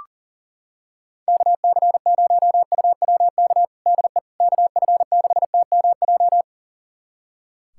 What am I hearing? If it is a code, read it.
Morse code, heard as a keyed tone, about 37 wpm, 700 Hz. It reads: KC0UWK DE KF6TMJ
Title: USA Callsign DE Callsign - 6 Characters - 40wpm